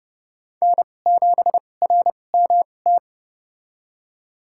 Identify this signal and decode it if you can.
Morse code, heard as a keyed tone, about 30 wpm, spaced as for 22 wpm, 700 Hz. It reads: N7RMT
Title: USA Call Signs - 2x Character Spacing - 30wpm